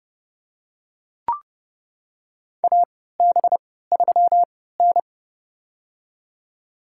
Morse code, heard as a keyed tone, about 30 wpm, spaced as for 17 wpm, 700 Hz. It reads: AB3N
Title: USA Call Signs - 4 Characters - 3x Character Spacing - 30wpm